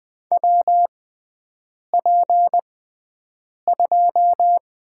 Morse code, heard as a keyed tone, about 20 wpm, spaced as for 7 wpm, 700 Hz. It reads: WP2
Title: USA Call Signs - 6 Characters - 6x Character Spacing - 20wpm